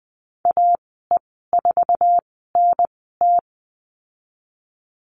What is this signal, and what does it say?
Morse code, heard as a keyed tone, about 20 wpm, spaced as for 15 wpm, 700 Hz. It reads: AE4NT
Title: 048 Lesson 9g - Number 4 - Call Signs 20wpm